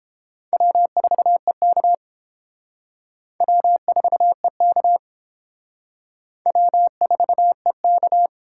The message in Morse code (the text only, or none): W4EK W4EK W4EK